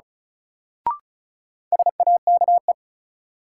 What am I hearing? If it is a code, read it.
Morse code, heard as a keyed tone, about 36 wpm, 700 Hz. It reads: SAKE
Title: Sets of 4 Words - New Word Drill - 35wpm